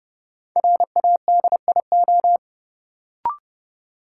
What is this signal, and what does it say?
Morse code, heard as a keyed tone, about 29 wpm, 700 Hz. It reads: RADIO
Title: Sentences from Top 1000 Words - Review Part 3 - New Word Drill - 30wpm